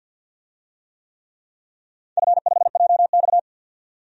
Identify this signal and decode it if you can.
Morse code, heard as a keyed tone, about 52 wpm, 700 Hz. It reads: F5JX
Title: DXCC-100 Call Signs - 4 Characters - 50wpm